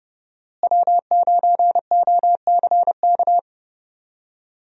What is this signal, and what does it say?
Morse code, heard as a keyed tone, about 30 wpm, 700 Hz. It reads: W9OCK
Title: USA Call Signs - 5 Characters - Repeated 2x - 30wpm